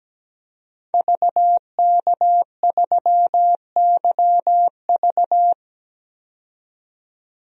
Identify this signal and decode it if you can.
Morse code, heard as a keyed tone, about 17 wpm, 700 Hz. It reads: VK3YV
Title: DXCC-100 Call Signs - 5 Characters - 17wpm